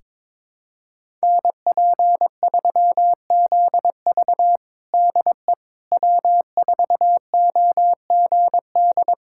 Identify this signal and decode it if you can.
Morse code, heard as a keyed tone, about 22 wpm, 700 Hz. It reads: NP3ZV DE W4OGD
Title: USA Callsign DE Callsign - 5 Characters - 22wpm